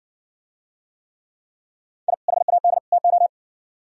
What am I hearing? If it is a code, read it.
Morse code, heard as a keyed tone, about 70 wpm, 700 Hz. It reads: I HAD TO